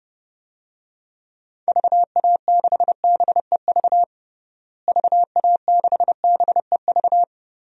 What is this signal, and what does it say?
Morse code, heard as a keyed tone, about 32 wpm, 700 Hz. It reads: VA6BEV VA6BEV
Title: DXCC-100 Call Signs - 6 Characters - Repeated 3x - 30wpm